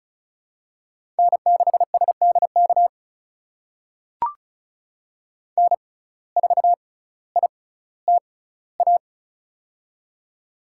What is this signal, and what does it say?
Morse code, heard as a keyed tone, about 36 wpm, spaced as for 18 wpm, 700 Hz. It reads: N6SDK N4ITA
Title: USA Call Signs - 5 Characters - 6x Character Spacing - 35wpm